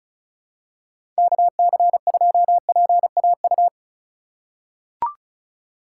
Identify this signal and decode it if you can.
Morse code, heard as a keyed tone, about 34 wpm, 700 Hz. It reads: KC2PAU
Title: USA Call Signs - 6 Characters - 4x Character Spacing - 35wpm